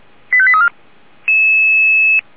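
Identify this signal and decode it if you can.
Five-tone selective call (ZVEI), heard as quick runs of five tones, tones of about 70 ms. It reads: 85834
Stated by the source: Five-tone ZVEI selective call received on an SDR.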